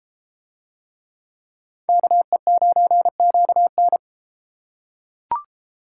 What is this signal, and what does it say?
Morse code, heard as a keyed tone, about 31 wpm, 700 Hz. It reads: KE9QN